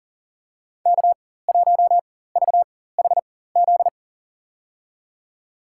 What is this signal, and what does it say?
Morse code, heard as a keyed tone, about 40 wpm, spaced as for 19 wpm, 700 Hz. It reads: K1VHZ
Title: USA Call Signs - 5 Characters - 4x Character Spacing - 40wpm